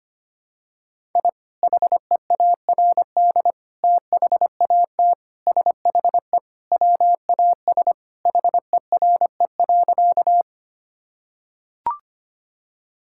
Morse code, heard as a keyed tone, about 25 wpm, 700 Hz. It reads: I HEARD THAT SHE WAS HERE.
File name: practice / 25wpm / Sentences with 3-Word Phrases - 25wpm